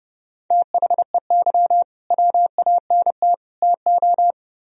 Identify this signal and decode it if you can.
Morse code, heard as a keyed tone, about 36 wpm, 700 Hz. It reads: THEY WANT TO
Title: Speed-Racing - Sets of 3 Words 30wpm